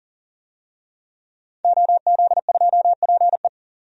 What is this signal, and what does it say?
Morse code, heard as a keyed tone, about 40 wpm, 700 Hz. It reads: OZ2PE
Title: DXCC-100 Call Signs - 5 Characters - 40wpm